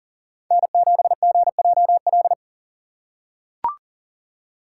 Morse code, heard as a keyed tone, about 40 wpm, 700 Hz. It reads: N7GJL
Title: USA Call Signs - 5 Characters - 4x Character Spacing - 40wpm